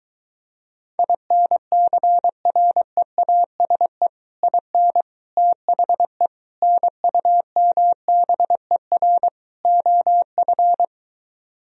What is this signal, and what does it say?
Morse code, heard as a keyed tone, about 23 wpm, 700 Hz. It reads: INCREASE IN THE NUMBER OF